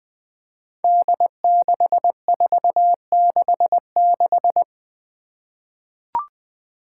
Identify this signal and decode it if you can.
Morse code, heard as a keyed tone, about 20 wpm, 700 Hz. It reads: D6466
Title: Five Challenging Letter-Number Combinations 20wpm